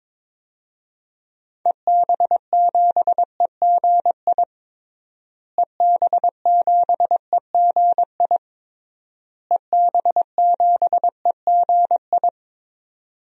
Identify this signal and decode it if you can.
Morse code, heard as a keyed tone, about 22 wpm, 700 Hz. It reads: EB7EGI EB7EGI EB7EGI
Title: DXCC-100 Call Signs - 6 Characters - Repeated 3x - 22wpm